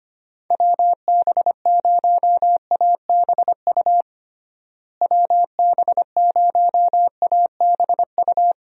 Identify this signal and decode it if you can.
Morse code, heard as a keyed tone, about 25 wpm, 700 Hz. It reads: WB0ABU WB0ABU
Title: USA Call Signs - 6 Characters - Repeated 3x - 25wpm